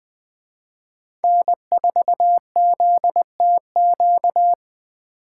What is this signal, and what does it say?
Morse code, heard as a keyed tone, about 20 wpm, 700 Hz. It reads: N4ZTQ
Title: Speed Racing - USA Call Signs 17wpm